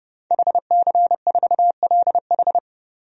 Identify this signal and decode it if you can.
Morse code, heard as a keyed tone, about 30 wpm, 700 Hz. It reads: HC4LH
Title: DXCC-100 Call Signs - 5 Characters - 30wpm